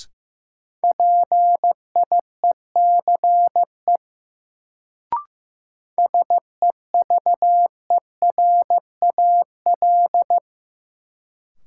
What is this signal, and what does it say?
Morse code, heard as a keyed tone, about 15 wpm, 700 Hz. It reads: PIECE SEVERAL
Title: Sentences from Top 400 Words - New Word Drill - 15wpm